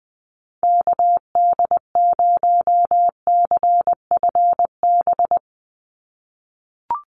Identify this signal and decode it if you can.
Morse code, heard as a keyed tone, about 20 wpm, 700 Hz. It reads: KD0CFB
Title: USA Call Signs 20wpm